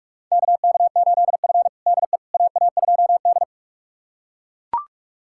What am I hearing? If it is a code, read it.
Morse code, heard as a keyed tone, about 41 wpm, 690 Hz. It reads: KK8F DE AA2D
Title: USA Callsign DE Callsign - 4 Characters - 45wpm